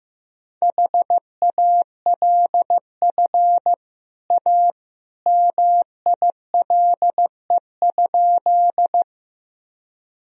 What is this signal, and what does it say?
Morse code, heard as a keyed tone, about 15 wpm, 700 Hz. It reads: HALF A MILE?
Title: Sentences from Top 1000 Words - Review Part 1 - 15wpm